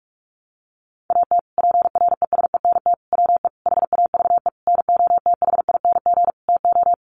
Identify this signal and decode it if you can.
Morse code, heard as a keyed tone, about 44 wpm, 700 Hz. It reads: AT PRESENT WE HAVE NOTHING TO
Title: Sentences from Top 800 Words 45wpm